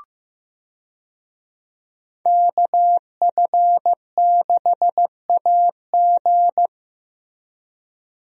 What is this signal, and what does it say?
Morse code, heard as a keyed tone, about 16 wpm, 700 Hz. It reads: KF6AG
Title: USA Call Signs - 5 Characters - Repeated 2x - 15wpm